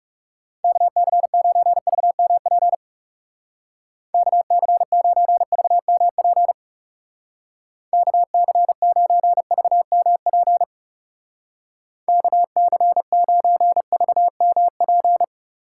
Morse code, heard as a keyed tone, about 35 wpm, 700 Hz. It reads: KC9VMP KC9VMP KC9VMP KC9VMP
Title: Speed Racing - USA 2x3 Call Signs 30wpm